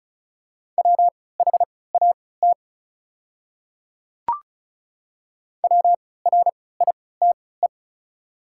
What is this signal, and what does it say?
Morse code, heard as a keyed tone, about 34 wpm, spaced as for 20 wpm, 690 Hz. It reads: WHAT WRITE
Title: 3x Character Spacing - Top 100 Words - 35wpm